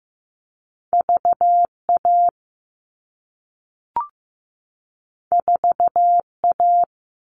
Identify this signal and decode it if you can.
Morse code, heard as a keyed tone, about 15 wpm, 700 Hz. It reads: VA 4A
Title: All Cut Numbers - 2-Digit 15wpm